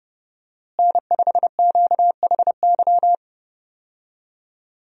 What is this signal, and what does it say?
Morse code, heard as a keyed tone, about 31 wpm, 700 Hz. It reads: N5QHY
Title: USA Call Signs - 5 Characters - Repeated 3x - 30wpm